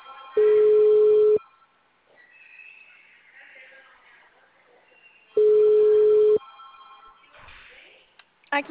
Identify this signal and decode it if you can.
Ringback tone (Europe, 425 Hz), with speech.